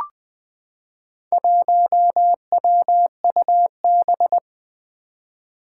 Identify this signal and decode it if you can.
Morse code, heard as a keyed tone, about 20 wpm, 700 Hz. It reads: 1WUB
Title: Four Challenging Letter-Number Combinations 20wpm